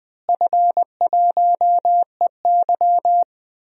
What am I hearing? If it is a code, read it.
Morse code, heard as a keyed tone, about 20 wpm, 700 Hz. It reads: F1EY
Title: DXCC-100 Call Signs - 4 Characters - Repeated 3x - 20wpm